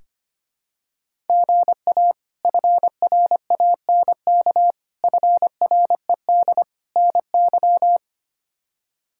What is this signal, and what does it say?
Morse code, heard as a keyed tone, about 25 wpm, 700 Hz. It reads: GA FRANK FRED NY
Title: K1USN SST QSO Practice - 25wpm